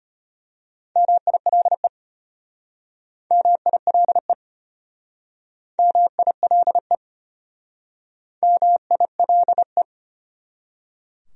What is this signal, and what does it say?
Morse code, heard as a keyed tone, about 33 wpm, 700 Hz. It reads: MILE MILE MILE MILE